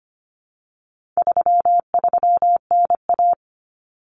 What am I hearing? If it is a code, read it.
Morse code, heard as a keyed tone, about 25 wpm, 700 Hz. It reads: 33NA
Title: Common Cut Numbers - 4-Digit 25wpm